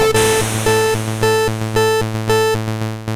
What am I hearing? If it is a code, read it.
There is no code.